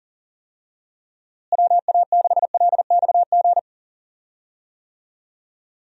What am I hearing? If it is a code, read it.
Morse code, heard as a keyed tone, about 40 wpm, 700 Hz. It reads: WA6LXG
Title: USA Call Signs - 6 Characters - Repeated 3x - 40wpm